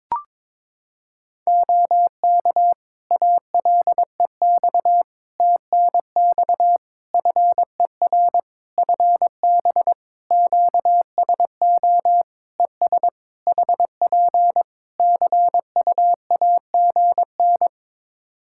Morse code, heard as a keyed tone, about 22 wpm, 700 Hz. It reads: OK ALEX TNX FER FB QSO ES HP CUAGN
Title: Practice QSO - Exchange 4 - Ending - 22wpm